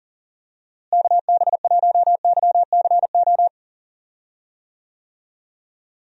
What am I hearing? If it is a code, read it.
Morse code, heard as a keyed tone, about 40 wpm, 700 Hz. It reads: KB1YCO
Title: USA Call Signs - 6 Characters - Repeated 2x - 40wpm